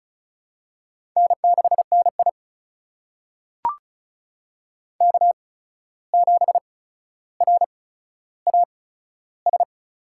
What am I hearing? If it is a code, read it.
Morse code, heard as a keyed tone, about 34 wpm, spaced as for 14 wpm, 700 Hz. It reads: N6NI K7RAS
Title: USA Call Signs - 8x Character Spacing - 35wpm